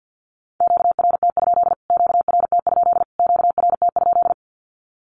Morse code, heard as a keyed tone, about 50 wpm, 700 Hz. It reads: QRT? QRT? QRT?
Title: Q-Codes Repeated 3x 50wpm